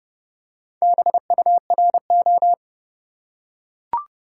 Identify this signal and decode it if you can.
Morse code, heard as a keyed tone, about 30 wpm, 700 Hz. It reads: BURO
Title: Top 180 QSO Elements 30wpm